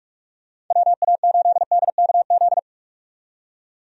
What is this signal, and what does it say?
Morse code, heard as a keyed tone, about 43 wpm, 700 Hz. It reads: WA8DKZ
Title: USA Call Signs - 6 Characters - 45wpm